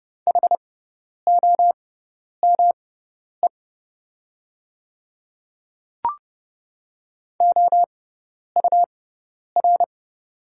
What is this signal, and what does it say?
Morse code, heard as a keyed tone, about 30 wpm, spaced as for 10 wpm, 690 Hz. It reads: HOME OUR
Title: 6x Character Spacing - Top 200 Words - 30wpm